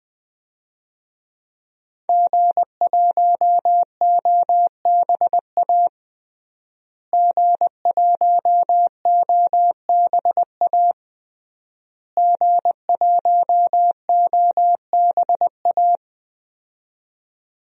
Morse code, heard as a keyed tone, about 20 wpm, 700 Hz. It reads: G1OBA G1OBA G1OBA
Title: DXCC-100 Call Signs - 5 Characters - Repeated 3x - 20wpm